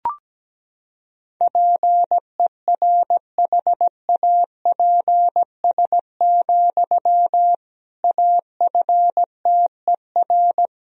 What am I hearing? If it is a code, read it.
Morse code, heard as a keyed tone, about 17 wpm, 700 Hz. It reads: PERHAPS, AFTER
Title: Sentences from Top 500 Words and Abbreviations 17wpm